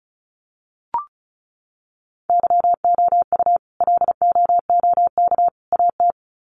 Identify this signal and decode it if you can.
Morse code, heard as a keyed tone, about 34 wpm, 700 Hz. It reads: YOU LOOK AT
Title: Sets of 3 Words 35wpm